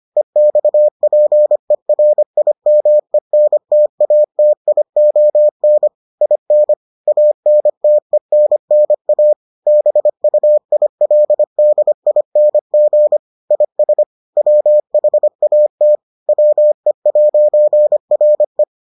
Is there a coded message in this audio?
Yes: Morse code from 0.2 to 18.6 s, heard as a keyed tone, about 25 wpm, 600 Hz. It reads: EXPERIMENTATION IN ANTENNA BUILDING IS WHAT WE'RE